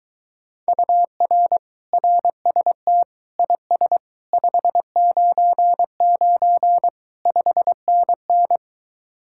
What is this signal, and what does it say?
Morse code, heard as a keyed tone, about 23 wpm, 700 Hz. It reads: UR RST IS 599 5NN